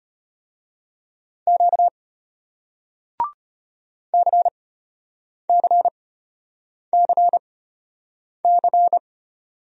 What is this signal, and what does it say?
Morse code, heard as a keyed tone, about 33 wpm, spaced as for 8 wpm, 700 Hz. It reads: Q CCCC